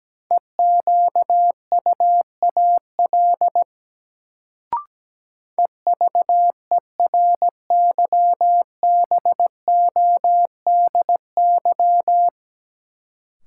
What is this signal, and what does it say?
Morse code, heard as a keyed tone, about 17 wpm, 700 Hz. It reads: EQUAL EVERYBODY
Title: Sentences from Top 1400 Words - New Word Drill - 17wpm